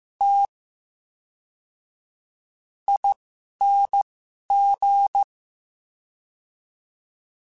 Morse code, heard as a keyed tone, about 15 wpm, spaced as for 11 wpm, 800 Hz. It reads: T ING
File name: Special INGs / Special INGs-mp3s / VerySlowToFaster Ing @10